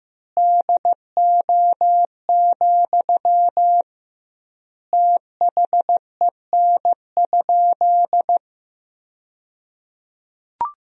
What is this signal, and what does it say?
Morse code, heard as a keyed tone, about 15 wpm, 700 Hz. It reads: DO, THEN?
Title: 2x Word-Spacing - Sentences from Top 100 Words - 15wpm